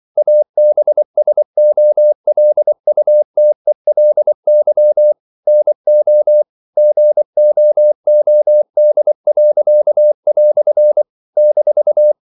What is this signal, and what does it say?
Morse code, heard as a keyed tone, about 24 wpm, 600 Hz. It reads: ABSOLUTELY NO GOOD." -
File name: CW / Practice / EnglishQuotations0224